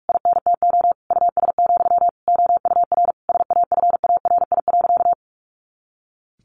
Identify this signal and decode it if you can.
Morse code, heard as a keyed tone, about 44 wpm, 700 Hz. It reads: INTO US, OUR SAFARI.
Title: A Sound Of Thunder 45wpm